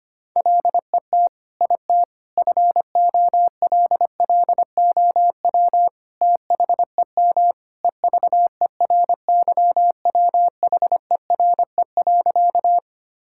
Morse code, heard as a keyed tone, about 25 wpm, 700 Hz. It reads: LET IT FOLLOW THEM EVERYWHERE.
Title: Sentences from Top 1600 Words 25wpm